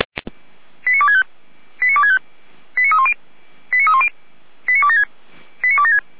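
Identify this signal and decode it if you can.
Five-tone selective call (ZVEI), heard as quick runs of five tones, tones of about 70 ms. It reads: 89265 89265 89310 89310 89276 89276
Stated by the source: Five-tone ZVEI selective call for the Orlamünde fire brigade, received on an SDR.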